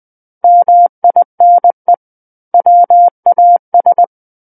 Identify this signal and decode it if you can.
Morse code, heard as a keyed tone, about 20 wpm, 700 Hz. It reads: MINE WAS
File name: 20 WPM